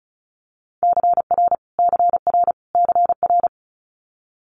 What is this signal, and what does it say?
Morse code, heard as a keyed tone, about 35 wpm, 700 Hz. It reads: CR CR CR
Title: Multi-letter Phonemes Repeated 3x 35wpm